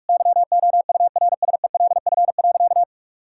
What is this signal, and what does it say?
Morse code, heard as a keyed tone, about 45 wpm, 700 Hz. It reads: YOURSELF.
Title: Sentences from Top 1000 Words - Review Part 5 - 45wpm